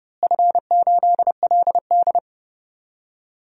Morse code, heard as a keyed tone, about 30 wpm, 700 Hz. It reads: F8LD
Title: DXCC-100 Call Signs - 4 Characters - Repeated 3x - 30wpm